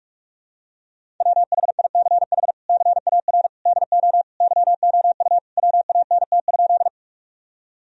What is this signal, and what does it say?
Morse code, heard as a keyed tone, about 44 wpm, 700 Hz. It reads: WHICH CAR DO YOU WANT?
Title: Sentences from Top 1000 Words - Review Part 1 - 45wpm